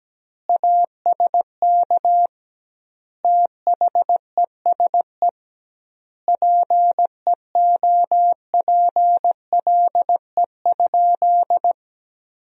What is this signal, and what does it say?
Morse code, heard as a keyed tone, about 17 wpm, 700 Hz. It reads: ASK THESE PEOPLE?
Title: 2x Word-Spacing - Sentences from Top 200 Words - 17wpm